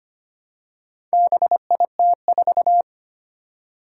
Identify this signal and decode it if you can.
Morse code, heard as a keyed tone, about 25 wpm, 700 Hz. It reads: BIT4
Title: Four Challenging Letter-Number Combinations 25wpm